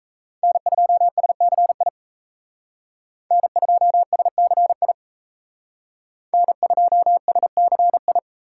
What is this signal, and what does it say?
Morse code, heard as a keyed tone, about 35 wpm, 700 Hz. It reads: N2SCI N2SCI N2SCI
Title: Speed Racing - USA 1x3 Call Signs 28wpm